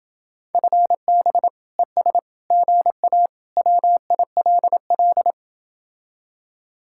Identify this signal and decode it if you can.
Morse code, heard as a keyed tone, about 30 wpm, 700 Hz. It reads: FB ES GA WILL